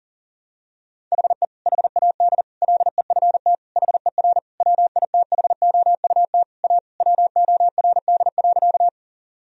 Morse code, heard as a keyed tone, about 40 wpm, 700 Hz. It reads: HE HAD LEFT HER WITHOUT A WORD.
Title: Sentences from Top 1000 Words - Review Part 3 - 40wpm